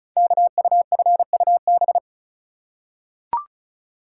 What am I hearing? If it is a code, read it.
Morse code, heard as a keyed tone, about 36 wpm, 690 Hz. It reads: KUFUB
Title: Code Groups - 5 Letters - 35wpm